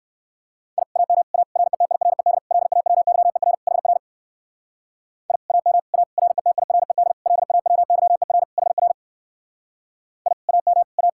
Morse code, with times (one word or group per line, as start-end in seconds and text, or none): 0.8-0.8 I
1.0-1.2 AM
1.3-1.4 A
1.6-2.4 RETIRED
2.5-3.5 LAWYER
3.7-4.0 HR
5.3-5.4 I
5.5-5.8 AM
5.9-6.0 A
6.2-7.1 RETIRED
7.3-8.4 LAWYER
8.6-8.9 HR
10.3-10.3 I
10.5-10.8 AM
11.0-11.1 A